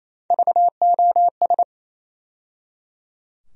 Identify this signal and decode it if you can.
Morse code, heard as a keyed tone, about 28 wpm, 700 Hz. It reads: VOS
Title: USA Call Sign Postfixes - 3 Letters 28wpm